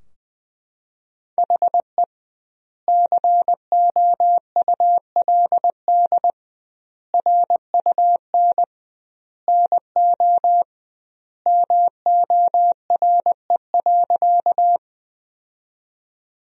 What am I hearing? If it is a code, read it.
Morse code, heard as a keyed tone, about 21 wpm, 700 Hz. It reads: HE COULD RUN NO MORE.
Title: 2x Word-Spacing - Sentences from Top 300 Words - 20wpm